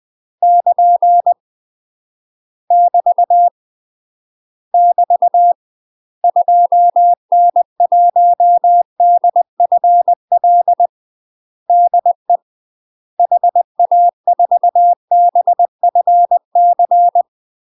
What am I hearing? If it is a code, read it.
Morse code, heard as a keyed tone, about 20 wpm, 700 Hz. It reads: <KN> <BT> <BT> 2N1DFL DE HA4BFC
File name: UK MASTER QSOs TRAINING / 20WPM / UK MASTER QSOs 0018